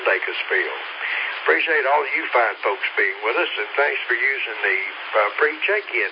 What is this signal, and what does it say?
Emergency Alert System (SAME) tones, heard as harsh buzzing bursts, with speech. It is heard from 1.0 to 5.0 s.